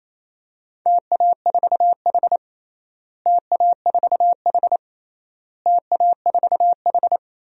Morse code, heard as a keyed tone, about 27 wpm, 700 Hz. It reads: TA4H TA4H TA4H
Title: DXCC-100 Call Signs - 4 Characters - Repeated 3x - 28wpm